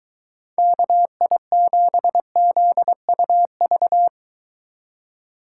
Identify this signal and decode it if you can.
Morse code, heard as a keyed tone, about 24 wpm, 700 Hz. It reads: KI7ZUV